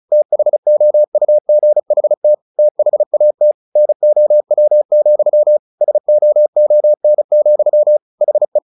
Morse code, heard as a keyed tone, about 34 wpm, 600 Hz. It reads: THOUGHT THAT NOW, SOON, HE
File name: the old man and the sea 2 35wpm